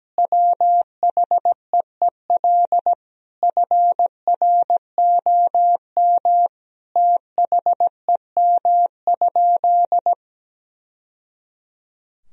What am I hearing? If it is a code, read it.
Morse code, heard as a keyed tone, about 17 wpm, 700 Hz. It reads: WHEEL FROM THEM?
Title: Sentences from Top 1000 Words - Review Part 3 - 17wpm